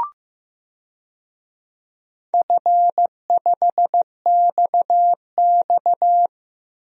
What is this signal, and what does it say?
Morse code, heard as a keyed tone, about 15 wpm, 700 Hz. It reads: F5XX
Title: DXCC-100 Call Signs - 4 Characters - Repeated 3x - 15wpm